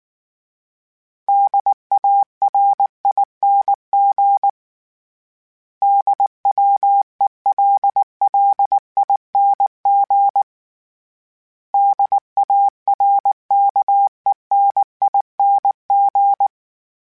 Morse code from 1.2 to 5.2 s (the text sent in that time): DARING